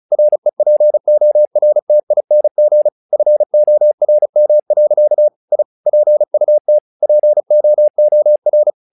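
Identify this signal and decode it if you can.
Morse code, heard as a keyed tone, about 37 wpm, 600 Hz. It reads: REPORTING FORM. I PUT POOR
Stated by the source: Morse practice text "Tower Trouble" sent at 35 WPM, from the SKCC - Straight Key Century Club learning center audio files.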